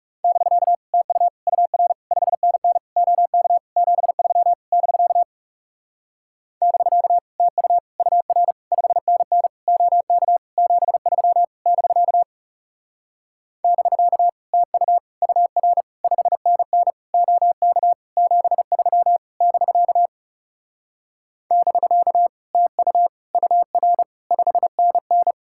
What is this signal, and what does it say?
Morse code, heard as a keyed tone, about 40 wpm, 700 Hz. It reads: <BK> TU UR 5NN OK 73 <BK> <BK> TU UR 5NN OK 73 <BK> <BK> TU UR 5NN OK 73 <BK> <BK> TU UR 5NN